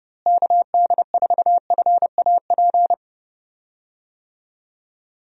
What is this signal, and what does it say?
Morse code, heard as a keyed tone, about 30 wpm, 700 Hz. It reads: KD4FAP
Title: USA Call Signs - 6 Characters - Repeated 2x - 30wpm